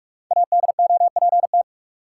Morse code, heard as a keyed tone, about 43 wpm, 700 Hz. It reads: ADOPT
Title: Five Letter Words - 45wpm